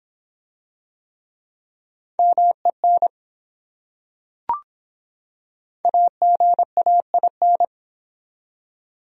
Morse code, heard as a keyed tone, about 24 wpm, 700 Hz. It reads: MEN AGAIN